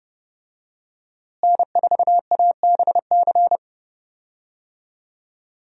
Morse code, heard as a keyed tone, about 30 wpm, 700 Hz. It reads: N4ABC
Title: USA Call Signs - 5 Characters - Repeated 3x - 30wpm